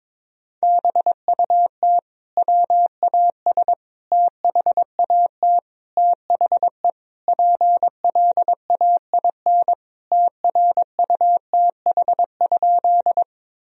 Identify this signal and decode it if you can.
Morse code, heard as a keyed tone, about 22 wpm, 700 Hz. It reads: BUT WAS THAT THE PLAIN TRUTH?
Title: Sentences from Top 1000 Words - Review Part 2 - 22wpm